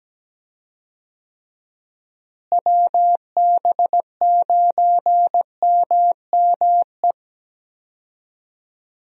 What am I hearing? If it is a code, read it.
Morse code, heard as a keyed tone, about 17 wpm, 700 Hz. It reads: WB9MME
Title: USA Call Signs - 6 Characters - Repeated 3x - 17wpm